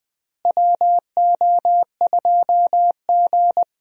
Morse code, heard as a keyed tone, about 20 wpm, 700 Hz. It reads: WO2G
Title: USA Call Signs - 4 Characters - Repeated 2x - 20wpm